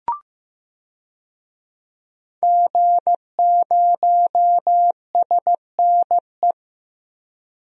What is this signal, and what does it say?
Morse code, heard as a keyed tone, about 15 wpm, 700 Hz. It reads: G0SNE
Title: DXCC-100 Call Signs - 5 Characters - 15wpm